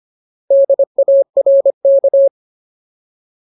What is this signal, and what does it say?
Morse code, heard as a keyed tone, about 25 wpm, 550 Hz. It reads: DARK
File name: Ultimate Word List / 25WPM Ultimate